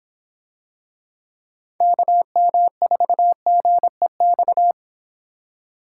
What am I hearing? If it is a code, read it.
Morse code, heard as a keyed tone, about 26 wpm, 700 Hz. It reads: KM4GEX